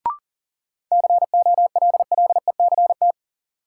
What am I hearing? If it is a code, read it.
Morse code, heard as a keyed tone, about 40 wpm, 700 Hz. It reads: COLLECT
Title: Sentences from Top 1600 Words - New Word Drill - 40wpm